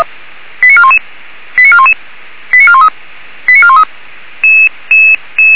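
Five-tone selective call (ZVEI), heard as quick runs of five tones, tones of about 70 ms. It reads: 89411 89411 89413 89413
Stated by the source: ZVEI five-tone selective call received by SDR.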